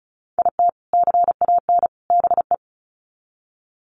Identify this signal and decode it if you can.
Morse code, heard as a keyed tone, about 34 wpm, 700 Hz. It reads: IT CAN BE
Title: Sets of 3 Words 35wpm